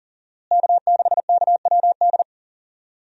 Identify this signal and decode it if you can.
Morse code, heard as a keyed tone, about 40 wpm, 700 Hz. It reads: K6KWD
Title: USA Call Signs - 5 Characters - Repeated 3x - 40wpm